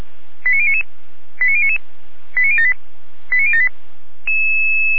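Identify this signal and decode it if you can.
Five-tone selective call (ZVEI), heard as quick runs of five tones, tones of about 70 ms. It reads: 89099 89099 89078 89078
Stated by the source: ZVEI five-tone call received with an SDR.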